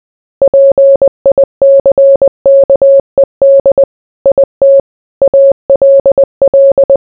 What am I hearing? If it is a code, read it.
Morse code, heard as a keyed tone, about 20 wpm, 550 Hz. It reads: PICKED IT ALL